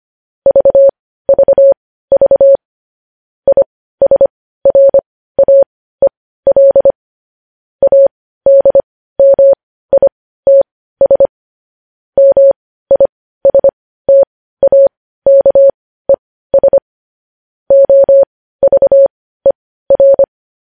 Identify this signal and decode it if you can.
Morse code, heard as a keyed tone, about 25 wpm, spaced as for 15 wpm, 550 Hz. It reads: VVV ISRAEL ADMITS MISTAKES OVER
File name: podcast / News headlines-1x-25x15